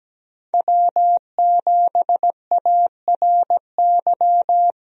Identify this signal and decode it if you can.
Morse code, heard as a keyed tone, about 17 wpm, 700 Hz. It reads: W7ARY
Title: USA Call Signs - 5 Characters - 17wpm